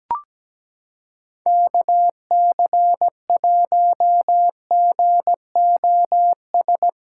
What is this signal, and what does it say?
Morse code, heard as a keyed tone, about 17 wpm, 700 Hz. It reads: KC1GOS
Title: USA Callsign DE Callsign - 6 Characters - 17wpm